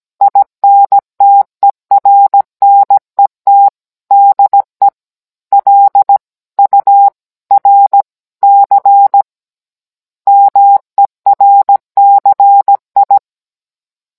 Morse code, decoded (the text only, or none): INTERNET DE L URC MERCI